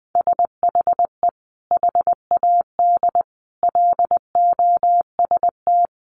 Morse code, heard as a keyed tone, about 20 wpm, 700 Hz. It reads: SHE HAD LOST
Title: Sentences from Top 1500 Words 20wpm